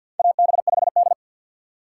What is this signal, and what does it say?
Morse code, heard as a keyed tone, about 50 wpm, 700 Hz. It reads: AB5D
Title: USA Call Signs - 4 Characters - 2x Character Spacing - 50wpm